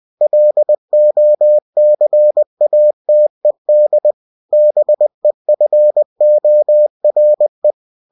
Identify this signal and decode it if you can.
Morse code, heard as a keyed tone, about 20 wpm, 600 Hz. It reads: LOCATED BEFORE
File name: winnie-20wpm